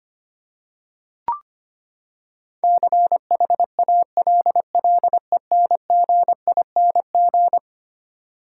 Morse code, heard as a keyed tone, about 25 wpm, 700 Hz. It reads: CHALLENGING
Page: morse-code-ninja-practice-25wpm